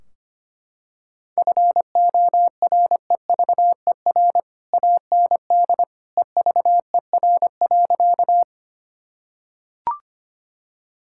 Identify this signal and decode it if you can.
Morse code, heard as a keyed tone, about 25 wpm, 700 Hz. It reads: FOREVER AND EVER.
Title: Sentences from Top 1600 Words 25wpm